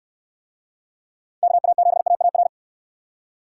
Morse code, heard as a keyed tone, about 62 wpm, 700 Hz. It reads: KA7AAR